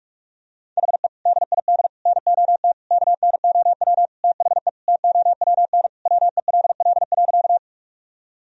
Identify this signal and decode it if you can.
Morse code, heard as a keyed tone, about 41 wpm, 700 Hz. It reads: HE DID NOT KNOW THE TOWN WELL.
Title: Sentences from Top 1000 Words - Review Part 2 - 45wpm